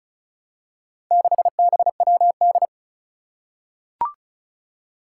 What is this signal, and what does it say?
Morse code, heard as a keyed tone, about 36 wpm, 700 Hz. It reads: 6BWD
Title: Four Challenging Letter-Number Combinations 35wpm